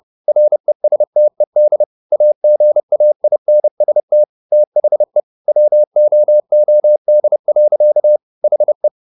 Morse code, heard as a keyed tone, about 30 wpm, 600 Hz. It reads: RESTED AGAINST THE WOOD. HE
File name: the old man and the sea / the old man and the sea 1 30wpm